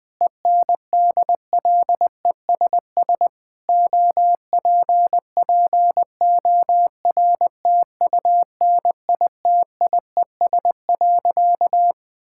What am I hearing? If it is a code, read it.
Morse code, heard as a keyed tone, about 21 wpm, 700 Hz. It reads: ENDLESS OPPORTUNITIES.
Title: Difficult Sentences with Hints - 20wpm